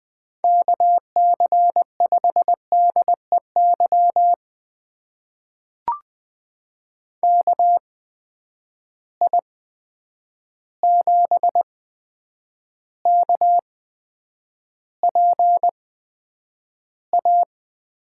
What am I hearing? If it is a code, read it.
Morse code, heard as a keyed tone, about 20 wpm, spaced as for 9 wpm, 700 Hz. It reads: KC5DEY KI7KPA
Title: USA Call Signs - 6 Characters - 8x Character Spacing - 20wpm